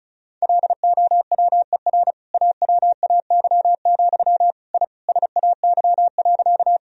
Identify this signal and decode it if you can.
Morse code, heard as a keyed tone, about 36 wpm, 700 Hz. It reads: LOWER AWAY, I SAY.
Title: Sentences from Top 1000 Words - Review Part 4 - 35wpm